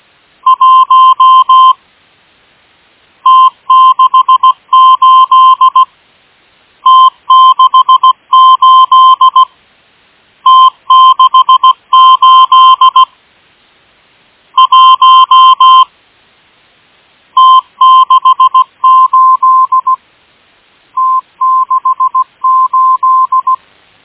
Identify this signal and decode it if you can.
Morse code, heard as a keyed tone, about 17 wpm, 1040 Hz. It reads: 1 T68 T68 T68 1 T68 T68
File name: M12- -1830utc-8157khz